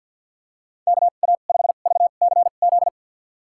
Morse code, heard as a keyed tone, about 49 wpm, spaced as for 35 wpm, 700 Hz. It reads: KA5VCZ
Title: USA Call Signs - 2x Character Spacing - 50wpm